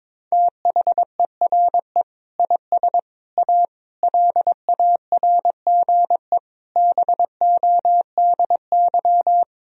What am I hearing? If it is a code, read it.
Morse code, heard as a keyed tone, about 22 wpm, 700 Hz. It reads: THERE IS A LARGE BODY
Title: Sentences from Top 1000 Words - Review Part 4 - 22wpm